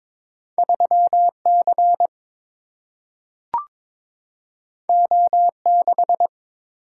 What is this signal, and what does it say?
Morse code, heard as a keyed tone, about 22 wpm, 700 Hz. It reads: 3 C O 6
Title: Single-Letter Single-Number 22wpm